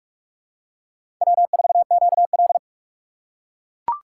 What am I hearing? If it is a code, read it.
Morse code, heard as a keyed tone, about 41 wpm, 700 Hz. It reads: W4QL
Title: USA Call Signs - 4 Characters - 3x Character Spacing - 45wpm